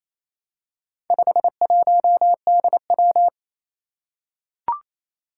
Morse code, heard as a keyed tone, about 29 wpm, 700 Hz. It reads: 51DW
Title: Four Challenging Letter-Number Combinations 28wpm